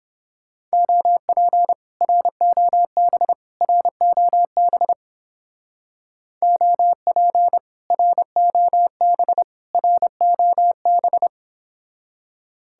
Morse code, heard as a keyed tone, about 29 wpm, 700 Hz. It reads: OP ROB ROB OP ROB ROB